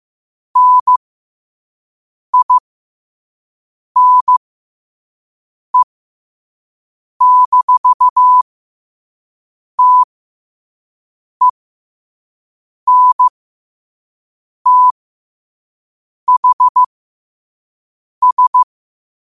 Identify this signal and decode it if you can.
Morse code, heard as a keyed tone, about 15 wpm, spaced as for 5 wpm, 1000 Hz. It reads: NINE-TENTHS
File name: qotdfiles / 5wpm